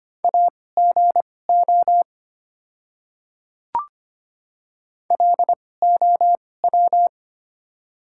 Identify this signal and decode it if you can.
Morse code, heard as a keyed tone, about 25 wpm, spaced as for 18 wpm, 690 Hz. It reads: AGO LOW
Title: 2x Character Spacing - Top 500 Words - 25wpm